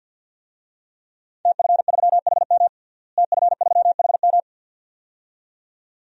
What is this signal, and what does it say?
Morse code, heard as a keyed tone, about 52 wpm, 700 Hz. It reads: TF3HM TF3HM
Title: DXCC-100 Call Signs - 5 Characters - Repeated 2x - 50wpm